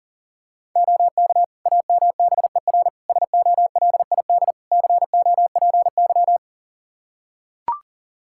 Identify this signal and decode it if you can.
Morse code, heard as a keyed tone, about 40 wpm, 690 Hz. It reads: OK AMBER SOLID COPY
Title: Practice QSO - Exchange 3 - More Info - 40wpm